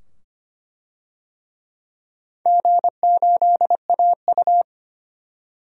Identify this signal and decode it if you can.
Morse code, heard as a keyed tone, about 25 wpm, 700 Hz. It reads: G8AU
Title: DXCC-100 Call Signs - 4 Characters - 25wpm